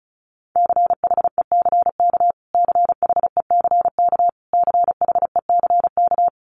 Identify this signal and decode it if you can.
Morse code, heard as a keyed tone, about 34 wpm, 700 Hz. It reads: CHECK CHECK CHECK
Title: Top 500 Words Repeated 35wpm